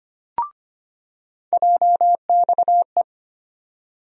Morse code, heard as a keyed tone, about 25 wpm, 700 Hz. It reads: JXE
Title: USA Call Sign Postfixes - 3 Letters 25wpm